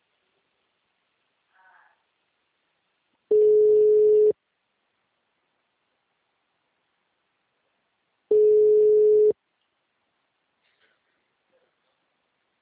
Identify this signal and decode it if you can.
Ringback tone (Europe, 425 Hz).